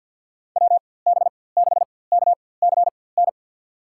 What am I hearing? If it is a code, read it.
Morse code, heard as a keyed tone, about 53 wpm, spaced as for 24 wpm, 700 Hz. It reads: WB6KCN
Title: USA Call Signs - 6 Characters - 4x Character Spacing - 50wpm